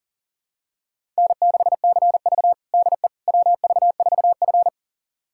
Morse code, heard as a keyed tone, about 40 wpm, 700 Hz. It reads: N6CV DE WV4F
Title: USA Callsign DE Callsign - 4 Characters - 40wpm